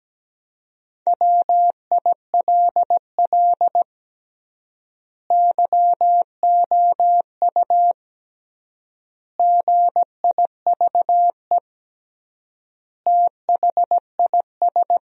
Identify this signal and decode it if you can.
Morse code, heard as a keyed tone, about 17 wpm, 700 Hz. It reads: WILL YOU GIVE THIS
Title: 3x Word-Spacing - Sentences from Top 200 Words - 17wpm